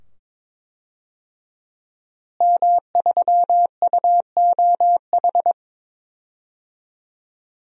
Morse code, heard as a keyed tone, about 22 wpm, 700 Hz. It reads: M3UOH
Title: DXCC-100 Call Signs - 5 Characters - Repeated 2x - 22wpm